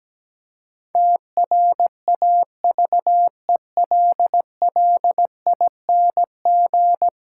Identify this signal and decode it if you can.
Morse code, heard as a keyed tone, about 17 wpm, 700 Hz. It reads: TRAVELLING
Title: Pangram Challenge v2 - 17wpm